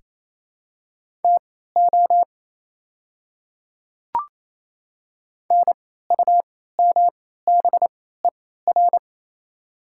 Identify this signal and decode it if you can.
Morse code, heard as a keyed tone, about 28 wpm, spaced as for 16 wpm, 700 Hz. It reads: TO NUMBER
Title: 3x Character Spacing - Top 100 Words - 28wpm